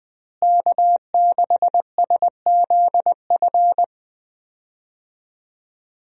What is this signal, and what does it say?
Morse code, heard as a keyed tone, about 20 wpm, 700 Hz. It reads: K6SZF
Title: USA Call Signs - 5 Characters - Repeated 3x - 20wpm